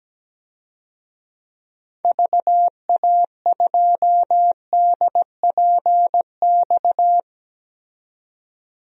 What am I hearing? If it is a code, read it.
Morse code, heard as a keyed tone, about 17 wpm, 700 Hz. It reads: VA2DPX
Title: DXCC-100 Call Signs - 6 Characters - 17wpm